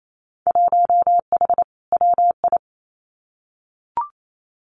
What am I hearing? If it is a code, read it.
Morse code, heard as a keyed tone, about 28 wpm, 700 Hz. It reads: 1H WI
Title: Winter Field Day Exchanges 28wpm